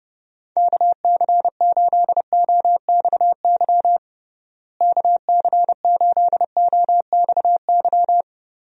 Morse code, heard as a keyed tone, about 30 wpm, 700 Hz. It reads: KC8OXY KC8OXY
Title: USA Call Signs - 6 Characters - Repeated 3x - 30wpm